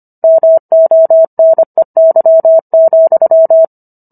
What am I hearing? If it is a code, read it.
Morse code, heard as a keyed tone, about 25 wpm, 650 Hz. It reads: MONEY,